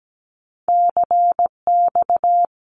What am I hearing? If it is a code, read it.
Morse code, heard as a keyed tone, about 17 wpm, 700 Hz. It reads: CX
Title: Abbreviations Repeated 3x - 17wpm